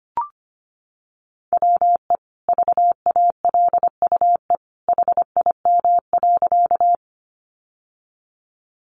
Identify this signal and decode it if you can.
Morse code, heard as a keyed tone, about 25 wpm, 700 Hz. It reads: WE VALUE HIM.
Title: Sentences from Top 600 Words 25wpm